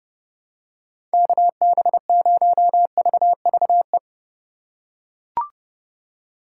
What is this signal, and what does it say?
Morse code, heard as a keyed tone, about 32 wpm, 700 Hz. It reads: KB0VVE
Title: USA Call Signs - 6 Characters - 4x Character Spacing - 30wpm